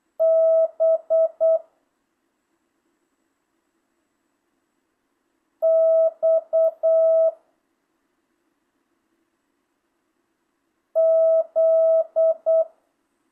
Morse code, heard as a keyed tone, about 8 wpm, 650 Hz. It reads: B X Z
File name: jh8cbh / 2025cw